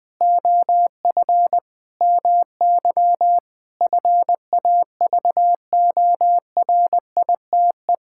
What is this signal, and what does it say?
Morse code, heard as a keyed tone, about 21 wpm, 700 Hz. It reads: OF MY FAVORITE
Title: Sentences with 4-Word Phrases - 20wpm